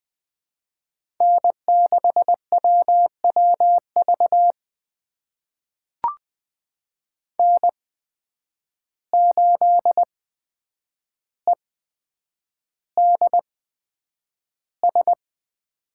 Morse code, heard as a keyed tone, about 20 wpm, spaced as for 9 wpm, 700 Hz. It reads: N6WWV N8EDS
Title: USA Call Signs - 8x Character Spacing - 20wpm